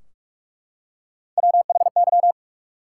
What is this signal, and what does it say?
Morse code, heard as a keyed tone, about 43 wpm, 700 Hz. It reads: WHY
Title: Three Letter Words - 45wpm